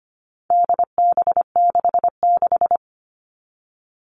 Morse code, heard as a keyed tone, about 25 wpm, 700 Hz. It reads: DB66
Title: All Cut Numbers - 4-Digit 25wpm